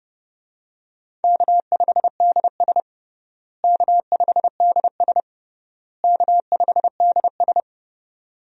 Morse code, heard as a keyed tone, about 30 wpm, 700 Hz. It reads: K5DS K5DS K5DS
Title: USA Call Signs - 4 Characters - Repeated 3x - 30wpm